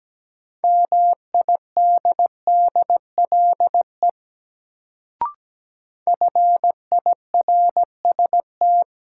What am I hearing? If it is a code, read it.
Morse code, heard as a keyed tone, about 17 wpm, 700 Hz. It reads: MIDDLE FIRST
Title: Sets of 2 Words - New Word Drill - 17wpm